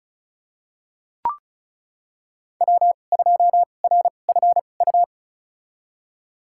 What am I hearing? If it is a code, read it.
Morse code, heard as a keyed tone, about 36 wpm, spaced as for 25 wpm, 700 Hz. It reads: W2RFU
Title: USA Call Signs - 5 Characters - 2x Character Spacing - 35wpm